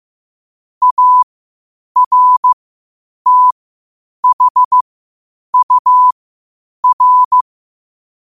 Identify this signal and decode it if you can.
Morse code, heard as a keyed tone, about 15 wpm, spaced as for 8 wpm, 1000 Hz. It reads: ARTHUR